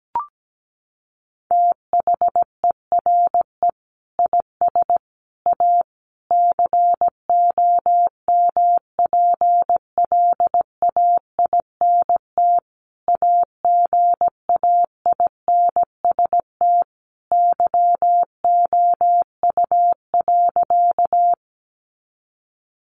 Morse code, heard as a keyed tone, about 17 wpm, 700 Hz. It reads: THERE IS A COMPLAINT AGAINST YOU.
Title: Sentences from Top 1400 Words 17wpm